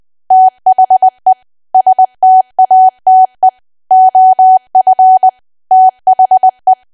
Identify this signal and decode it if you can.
Morse code, heard as a keyed tone, about 20 wpm, 750 Hz. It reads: THE STATE OF THE
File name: cwa / W1AW-Level3 / 20WPM